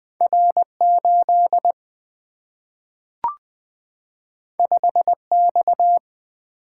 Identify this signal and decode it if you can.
Morse code, heard as a keyed tone, about 20 wpm, 700 Hz. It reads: R8 5X
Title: Single-Letter Single-Number 20wpm